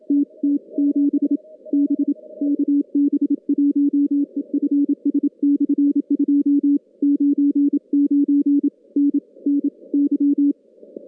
Morse code, heard as a keyed tone, about 27 wpm, 280 Hz. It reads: TT7 B KB1EFS/2 99 NNY
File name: contests / CW-SS-k5tr- -2200